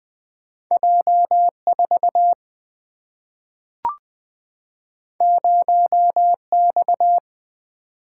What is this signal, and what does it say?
Morse code, heard as a keyed tone, about 20 wpm, 700 Hz. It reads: J4 0X
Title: Single-Letter Single-Number 20wpm